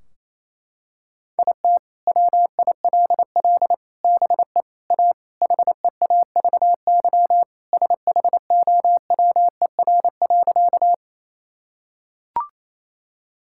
Morse code, heard as a keyed tone, about 28 wpm, 700 Hz. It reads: IT WILL BE A HEAVY SHOWER.
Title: Sentences from Top 1600 Words 28wpm